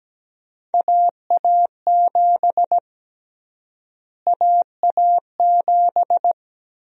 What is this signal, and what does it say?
Morse code, heard as a keyed tone, about 17 wpm, 700 Hz. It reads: AA7 AA7
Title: USA Call Sign Prefixes - Repeated 2x - 17wpm